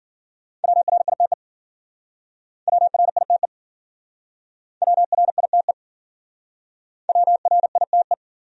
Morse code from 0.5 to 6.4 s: WRITE WRITE WRITE

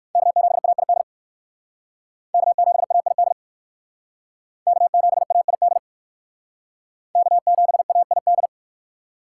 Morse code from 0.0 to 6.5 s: K7AID K7AID K7AID

